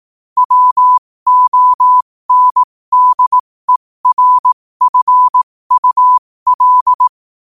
Morse code, heard as a keyed tone, about 18 wpm, spaced as for 15 wpm, 1000 Hz. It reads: WONDERFUL
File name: qotdfiles / 15wpm